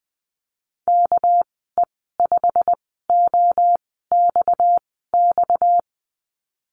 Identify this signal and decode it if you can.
Morse code, heard as a keyed tone, about 20 wpm, spaced as for 14 wpm, 700 Hz. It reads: KE5OXX
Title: 245 Lesson 39g - Letter z - Call Signs 20wpm